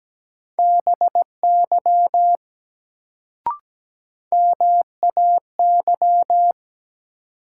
Morse code, heard as a keyed tone, about 17 wpm, 700 Hz. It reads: BY MAY
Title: Sets of 2 Words - New Word Drill - 17wpm